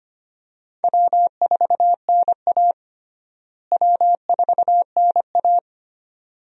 Morse code, heard as a keyed tone, about 25 wpm, 700 Hz. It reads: W4NA W4NA
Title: USA Call Signs - 4 Characters - Repeated 2x - 25wpm